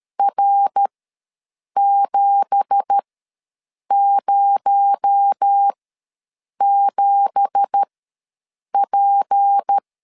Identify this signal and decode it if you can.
Morse code, heard as a keyed tone, about 13 wpm, spaced as for 7 wpm, 800 Hz. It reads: R707P
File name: LEZIONI / 9 2867 ws 12wpm 3431